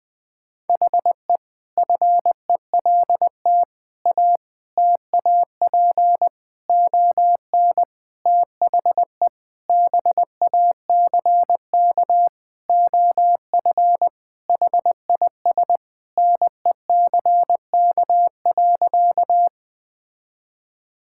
Morse code, heard as a keyed tone, about 21 wpm, 700 Hz. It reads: HE FELT A TAP ON THE BACK OF HIS NECK.
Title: Sentences with 4-Word Phrases - 20wpm